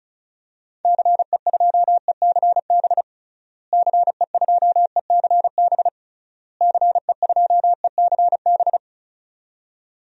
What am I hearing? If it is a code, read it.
Morse code, heard as a keyed tone, about 34 wpm, 700 Hz. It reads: CE2ECB CE2ECB CE2ECB
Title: DXCC-100 Call Signs - 6 Characters - Repeated 3x - 35wpm